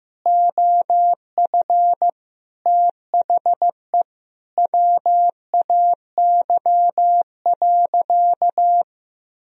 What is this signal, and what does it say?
Morse code, heard as a keyed tone, about 15 wpm, 700 Hz. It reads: OF THE WAY.
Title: Sentences from Top 1000 Words - Review Part 4 - 15wpm